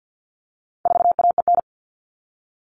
Morse code, heard as a keyed tone, about 50 wpm, 700 Hz. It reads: 4AEN
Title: All Cut Numbers - 4-Digit 50wpm